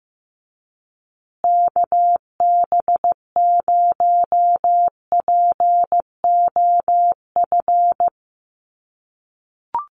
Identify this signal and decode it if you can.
Morse code, heard as a keyed tone, about 15 wpm, 700 Hz. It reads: KB0POF
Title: USA Call Signs Repeated 2x 15wpm